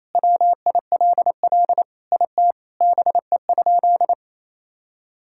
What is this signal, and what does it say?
Morse code, heard as a keyed tone, about 27 wpm, 700 Hz. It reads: WILL IT BE?
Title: Sentences from Top 1000 Words - Review Part 5 - 28wpm